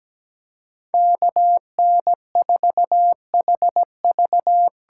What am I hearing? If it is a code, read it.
Morse code, heard as a keyed tone, about 17 wpm, 700 Hz. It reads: KN4HV
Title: USA Call Signs - 8x Character Spacing - 17wpm